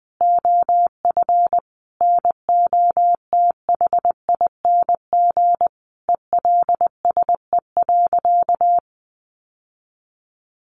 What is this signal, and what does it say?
Morse code, heard as a keyed tone, about 20 wpm, 700 Hz. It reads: OF NOTHING ELSE.
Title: Sentences from Top 800 Words 20wpm